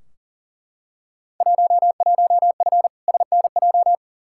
Morse code, heard as a keyed tone, about 40 wpm, 700 Hz. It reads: 11F SNJ
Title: Field Day Exchanges Encore - 40wpm